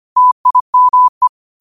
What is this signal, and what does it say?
Morse code, heard as a keyed tone, about 25 wpm, 1000 Hz. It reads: TIME
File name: qotdfiles / 25wpm